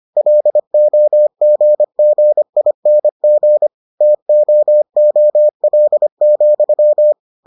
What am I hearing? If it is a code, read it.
Morse code, heard as a keyed tone, about 25 wpm, 600 Hz. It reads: LOGGING TOOL,